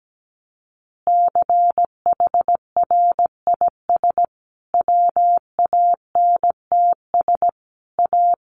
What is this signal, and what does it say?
Morse code, heard as a keyed tone, about 17 wpm, 700 Hz. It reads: CHRIS WANTS A
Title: Additional Easy Phrases 17wpm